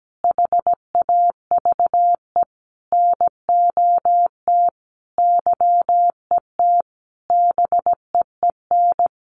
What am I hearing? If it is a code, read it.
Morse code, heard as a keyed tone, about 17 wpm, 700 Hz. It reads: HAVE NOT YET BEEN
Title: Sentences from Top 700 Words 17wpm